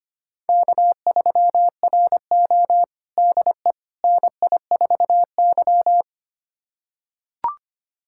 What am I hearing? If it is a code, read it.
Morse code, heard as a keyed tone, about 25 wpm, 700 Hz. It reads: K3RO DE NI4Y
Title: USA Callsign DE Callsign - 4 Characters - 25wpm